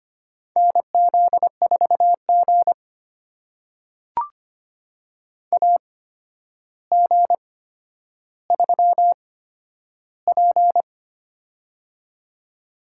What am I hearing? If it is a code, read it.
Morse code, heard as a keyed tone, about 25 wpm, spaced as for 11 wpm, 700 Hz. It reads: NZ4G AG3P
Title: USA Call Signs - 4 Characters - 8x Character Spacing - 25wpm